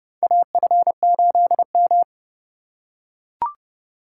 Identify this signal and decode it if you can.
Morse code, heard as a keyed tone, about 31 wpm, 700 Hz. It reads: AF8M
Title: USA Call Signs - 4 Characters - 4x Character Spacing - 30wpm